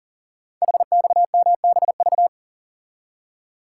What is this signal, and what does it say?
Morse code, heard as a keyed tone, about 39 wpm, 700 Hz. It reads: HXMBV
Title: Code Groups - 5 Letters - 40wpm